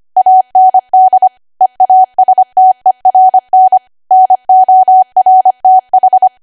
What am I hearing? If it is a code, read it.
Morse code, heard as a keyed tone, about 25 wpm, 750 Hz. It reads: AND EASTERN NORTH